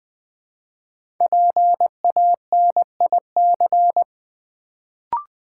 Morse code, heard as a keyed tone, about 20 wpm, 700 Hz. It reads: PANIC
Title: Build A Word - 1 Letter At A Time - 3 to 5-Letter Words - 20wpm